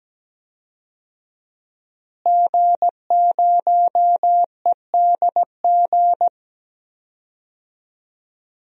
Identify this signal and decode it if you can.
Morse code, heard as a keyed tone, about 17 wpm, 700 Hz. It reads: G0EDG
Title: DXCC-100 Call Signs - 5 Characters - Repeated 2x - 17wpm